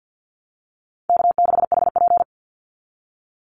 Morse code, heard as a keyed tone, about 50 wpm, 700 Hz. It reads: K6HP
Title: Call Signs Repeated 2x - 50wpm